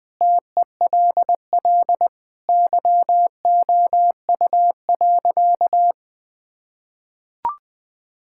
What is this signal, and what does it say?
Morse code, heard as a keyed tone, about 21 wpm, 700 Hz. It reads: TELL YOU.
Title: Sentences from Top 1000 Words - Review Part 5 - 20wpm